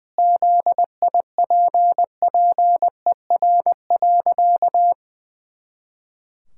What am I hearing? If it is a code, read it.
Morse code, heard as a keyed tone, about 20 wpm, 700 Hz. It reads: ZIPPER.
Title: Pangram Challenge v2 - 20wpm